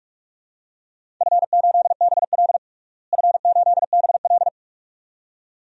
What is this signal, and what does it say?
Morse code, heard as a keyed tone, about 42 wpm, 700 Hz. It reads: F8BL F8BL
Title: DXCC-100 Call Signs - 4 Characters - Repeated 2x - 45wpm